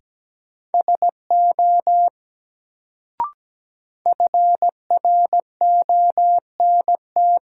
Morse code, heard as a keyed tone, about 17 wpm, 700 Hz. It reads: SO FRONT
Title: Sets of 2 Words - New Word Drill - 17wpm